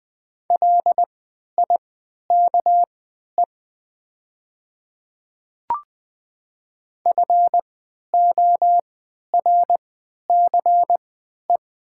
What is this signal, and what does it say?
Morse code, heard as a keyed tone, about 20 wpm, spaced as for 11 wpm, 700 Hz. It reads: LIKE FORCE
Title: 3x Character Spacing - Top 500 Words - 20wpm